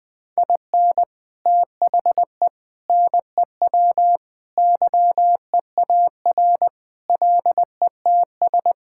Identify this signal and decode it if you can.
Morse code, heard as a keyed tone, about 20 wpm, 700 Hz. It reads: IN THE NEW YEAR LETS
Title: Sentences with 3-Word Phrases - Spoken Beginning - 20wpm